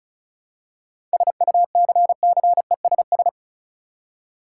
Morse code, heard as a keyed tone, about 34 wpm, 700 Hz. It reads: SUCCESS
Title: Seven Letter Words - 35wpm